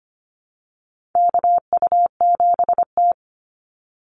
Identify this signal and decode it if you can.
Morse code, heard as a keyed tone, about 25 wpm, 700 Hz. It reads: KU7T
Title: Call Signs Repeated 3x - 25wpm